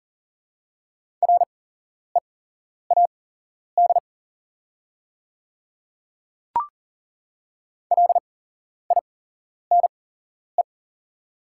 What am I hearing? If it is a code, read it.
Morse code, heard as a keyed tone, about 37 wpm, spaced as for 11 wpm, 700 Hz. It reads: READ LINE
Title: 8x Character Spacing - Top 200 Words - 40wpm